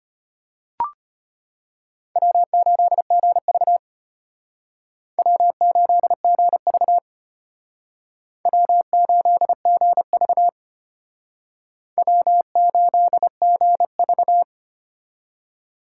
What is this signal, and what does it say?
Morse code, heard as a keyed tone, about 32 wpm, 700 Hz. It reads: W8GV W8GV W8GV W8GV